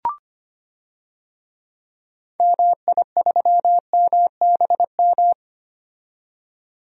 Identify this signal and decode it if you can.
Morse code, heard as a keyed tone, about 25 wpm, 700 Hz. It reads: MI3MBM
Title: DXCC-100 Call Signs - 6 Characters - 25wpm